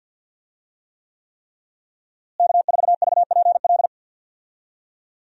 Morse code, heard as a keyed tone, about 52 wpm, 700 Hz. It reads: K4VPL
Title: USA Call Signs - 5 Characters - Repeated 3x - 50wpm